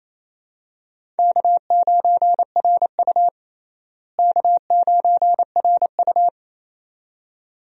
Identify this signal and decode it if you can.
Morse code, heard as a keyed tone, about 28 wpm, 700 Hz. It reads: K9RU K9RU
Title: USA Call Signs - 4 Characters - Repeated 2x - 28wpm